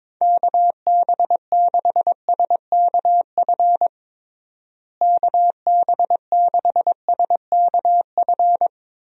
Morse code, heard as a keyed tone, about 22 wpm, 700 Hz. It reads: KB6SKF KB6SKF
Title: USA Call Signs - 6 Characters - Repeated 2x - 22wpm